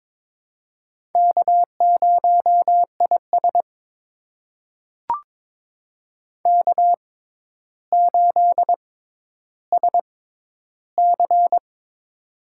Morse code, heard as a keyed tone, about 22 wpm, spaced as for 11 wpm, 700 Hz. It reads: K0IS K8SC
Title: USA Call Signs - 6x Character Spacing - 22wpm